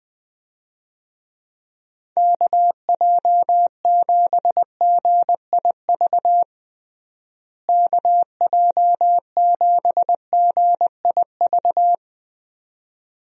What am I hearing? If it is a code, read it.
Morse code, heard as a keyed tone, about 20 wpm, 700 Hz. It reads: KJ7GIV KJ7GIV
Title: USA Call Signs - 6 Characters - Repeated 2x - 20wpm